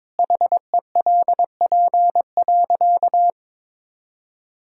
Morse code, heard as a keyed tone, about 22 wpm, 700 Hz. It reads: HELP.
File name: practice / 22wpm / Sentences with 3-Word Phrases - 22wpm